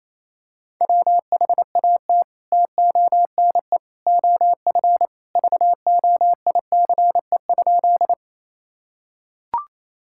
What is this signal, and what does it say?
Morse code, heard as a keyed tone, about 28 wpm, 700 Hz. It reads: WHAT TONE OF VOICE?
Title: Sentences from Top 1600 Words 28wpm